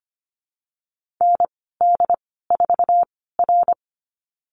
Morse code, heard as a keyed tone, about 25 wpm, spaced as for 16 wpm, 700 Hz. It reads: ND4R
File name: Lesson 15g - Number 5 - Call Signs 25wpm